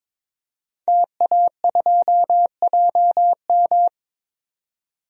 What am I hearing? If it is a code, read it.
Morse code, heard as a keyed tone, about 22 wpm, 700 Hz. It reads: TA2JM
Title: DXCC-100 Call Signs - 5 Characters - 22wpm